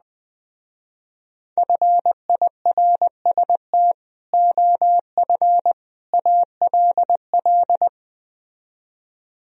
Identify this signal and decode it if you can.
Morse code, heard as a keyed tone, about 20 wpm, 700 Hz. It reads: FIRST OF ALL
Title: Speed-Racing - Sets of 3 Words 17wpm